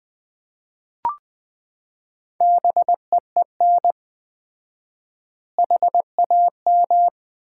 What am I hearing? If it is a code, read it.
Morse code, heard as a keyed tone, about 20 wpm, 700 Hz. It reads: BEEN HAM
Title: Common QSO Phrases - 4x Word-Spacing - 20wpm